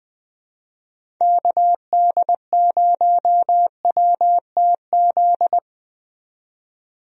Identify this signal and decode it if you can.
Morse code, heard as a keyed tone, about 20 wpm, 700 Hz. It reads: KD0WTZ
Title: USA Call Signs - 6 Characters - 20wpm